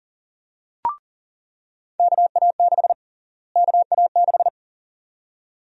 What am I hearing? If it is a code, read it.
Morse code, heard as a keyed tone, about 40 wpm, 700 Hz. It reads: KA6 KA6
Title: USA Call Sign Prefixes - Repeated 2x - 40wpm